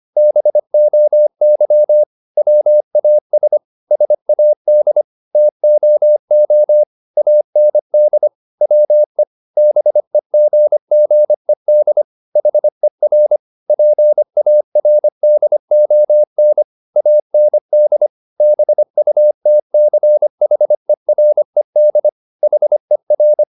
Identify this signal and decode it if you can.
Morse code, heard as a keyed tone, about 25 wpm, 600 Hz. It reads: BOY WAS SAD TOO AND WE BEGGED HER PARDON AND BUTCHERED HER
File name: the old man and the sea part 1 25wpm